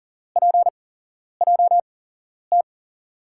Morse code, heard as a keyed tone, about 40 wpm, spaced as for 11 wpm, 700 Hz. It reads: PJT
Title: USA Call Signs - 6 Characters - 8x Character Spacing - 40wpm